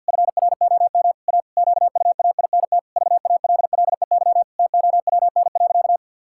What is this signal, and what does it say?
Morse code, heard as a keyed tone, about 51 wpm, 700 Hz. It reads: FROM A QUAINT VALLEY TOWN.
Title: Pangram Challenge v2 - 50wpm